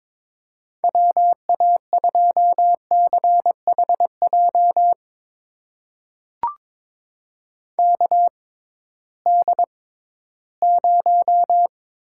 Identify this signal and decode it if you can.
Morse code, heard as a keyed tone, about 22 wpm, spaced as for 14 wpm, 700 Hz. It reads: WA2CHJ KD0
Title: USA Call Signs - 6 Characters - 6x Character Spacing - 22wpm